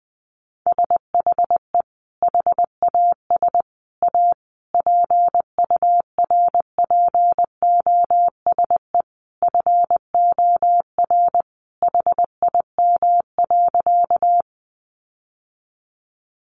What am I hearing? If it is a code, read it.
Morse code, heard as a keyed tone, about 19 wpm, 700 Hz. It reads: SHE HAS A PURPOSE FOR HIM.
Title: Sentences from Top 800 Words 20wpm